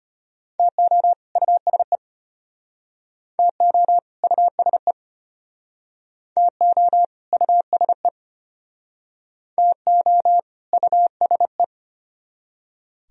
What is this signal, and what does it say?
Morse code, heard as a keyed tone, about 34 wpm, 700 Hz. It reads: TO USE TO USE TO USE TO USE